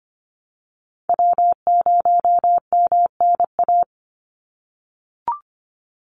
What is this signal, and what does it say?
Morse code, heard as a keyed tone, about 25 wpm, 700 Hz. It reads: W0MNA
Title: NA SOTA Chaser Callsigns 25wpm